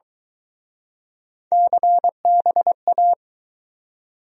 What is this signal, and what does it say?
Morse code, heard as a keyed tone, about 22 wpm, 700 Hz. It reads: CBA